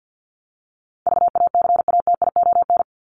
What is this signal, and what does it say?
Morse code, heard as a keyed tone, about 50 wpm, 700 Hz. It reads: VACATION
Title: Top 1501-1750 Words 50wpm